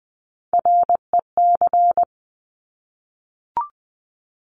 Morse code, heard as a keyed tone, about 19 wpm, 700 Hz. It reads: REC
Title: Top Three Letter Combinations in Words 20wpm